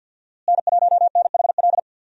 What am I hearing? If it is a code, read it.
Morse code, heard as a keyed tone, about 50 wpm, 700 Hz. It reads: N1NHL
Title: USA Call Signs - 5 Characters - Repeated 2x - 50wpm